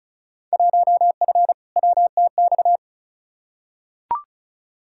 Morse code, heard as a keyed tone, about 34 wpm, 700 Hz. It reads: 1F WTX
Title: Field Day Exchanges - 35wpm